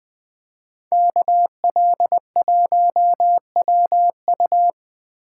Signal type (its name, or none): Morse code